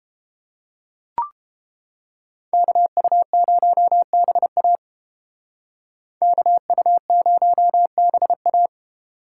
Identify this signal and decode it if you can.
Morse code, heard as a keyed tone, about 30 wpm, 700 Hz. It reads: KU0BA KU0BA